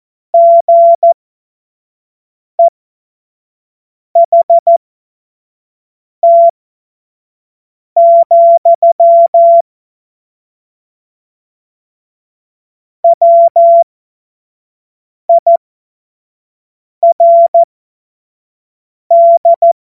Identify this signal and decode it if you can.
Morse code, heard as a keyed tone, about 14 wpm, spaced as for 5 wpm, 680 Hz. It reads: GEHT, WIRD
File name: Morsekurs / Witze / Witz 18. Mai 2021, 25 F70